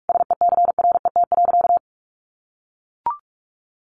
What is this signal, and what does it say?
Morse code, heard as a keyed tone, about 44 wpm, 700 Hz. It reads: SECRET.
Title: Sentences from Top 1200 Words 45wpm